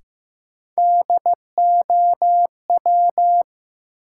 Morse code, heard as a keyed tone, about 15 wpm, 700 Hz. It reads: DOW
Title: Sets of 5 Words - New Word Drill - 15wpm